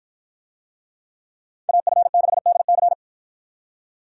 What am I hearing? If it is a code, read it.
Morse code, heard as a keyed tone, about 53 wpm, 700 Hz. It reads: AF6DC